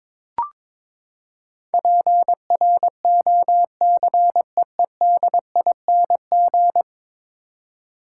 Morse code, heard as a keyed tone, about 22 wpm, 700 Hz. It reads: PROCEEDING
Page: morse-code-ninja-practice-22wpm